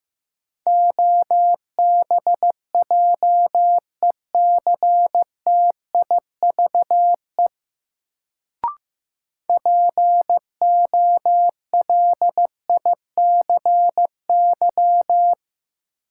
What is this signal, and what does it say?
Morse code, heard as a keyed tone, about 15 wpm, 700 Hz. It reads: OBJECTIVE POLICY
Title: Sentences from Top 1200 Words - New Word Drill Alt - 15wpm